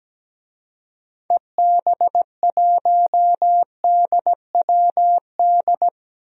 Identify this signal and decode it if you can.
Morse code, heard as a keyed tone, about 17 wpm, 700 Hz. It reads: EB1DWD
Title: DXCC-100 Call Signs - 6 Characters - Repeated 3x - 17wpm